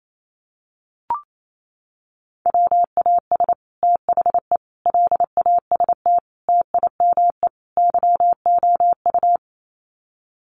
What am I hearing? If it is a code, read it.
Morse code, heard as a keyed tone, about 28 wpm, 700 Hz. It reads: WAS THE LAST TIME YOU
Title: Sets of 5 Words Encore2 28wpm